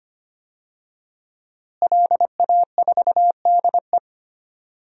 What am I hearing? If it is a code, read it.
Morse code, heard as a keyed tone, about 25 wpm, 700 Hz. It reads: LA4DE
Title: DXCC-100 Call Signs - 5 Characters - Repeated 2x - 25wpm